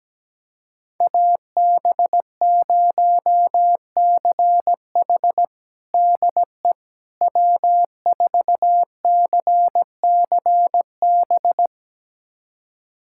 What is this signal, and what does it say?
Morse code, heard as a keyed tone, about 17 wpm, 700 Hz. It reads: AB0CH DE W4CCB
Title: USA Callsign DE Callsign - 5 Characters - 17wpm